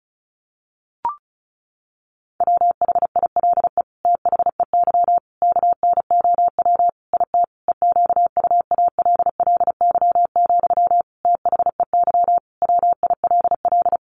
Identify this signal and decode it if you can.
Morse code, heard as a keyed tone, about 34 wpm, 700 Hz. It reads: WHILE THEY KNOW IT EQUALLY, THEY WILL
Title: Sentences from Top 1400 Words 35wpm